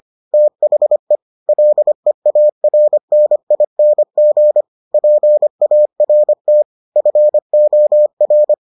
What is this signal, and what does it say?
Morse code, heard as a keyed tone, about 25 wpm, 600 Hz. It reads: THE LEARNING PART FOR